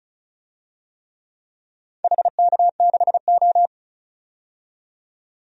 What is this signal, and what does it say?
Morse code, heard as a keyed tone, about 36 wpm, 700 Hz. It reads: HK6O
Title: DXCC-100 Call Signs - 4 Characters - 35wpm